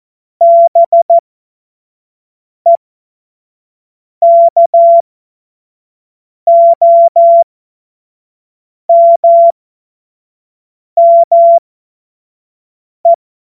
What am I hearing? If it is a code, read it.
Morse code, heard as a keyed tone, about 14 wpm, spaced as for 5 wpm, 680 Hz. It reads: BEKOMME
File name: Morsekurs / Witze / Witz 10. Januar 2023 25 F70